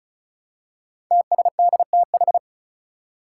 Morse code, heard as a keyed tone, about 36 wpm, 700 Hz. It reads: TSDTH
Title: Five Challenging Letter-Number Combinations 35wpm